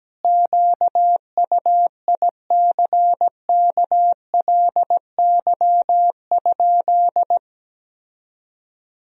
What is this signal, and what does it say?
Morse code, heard as a keyed tone, about 17 wpm, 700 Hz. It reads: QUICKLY?
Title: Sentences from Top 1000 Words - Review Part 3 - 17wpm